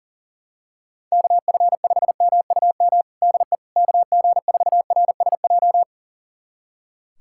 Morse code, heard as a keyed tone, about 37 wpm, 700 Hz. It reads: KF5MUM DE KG4RSJ
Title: USA Callsign DE Callsign - 6 Characters - 40wpm